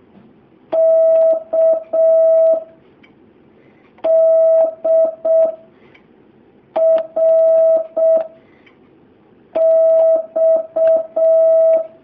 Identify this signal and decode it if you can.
Morse code, heard as a keyed tone, about 6 wpm, 650 Hz. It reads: K D R X
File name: jh8cbh / 2019cw